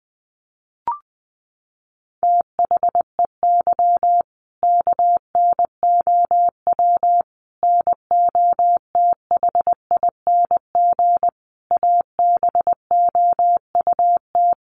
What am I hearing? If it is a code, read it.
Morse code, heard as a keyed tone, about 20 wpm, 700 Hz. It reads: THEY KNOW NOTHING ABOUT
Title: Sentences from Top 500 Words 20wpm